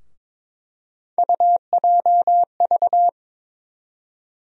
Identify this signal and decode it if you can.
Morse code, heard as a keyed tone, about 22 wpm, 700 Hz. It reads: UJV
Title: USA Call Sign Postfixes - 3 Letters 22wpm